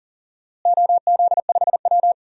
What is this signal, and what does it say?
Morse code, heard as a keyed tone, about 40 wpm, 700 Hz. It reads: OZ5W
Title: DXCC-100 Call Signs - 4 Characters - Repeated 3x - 40wpm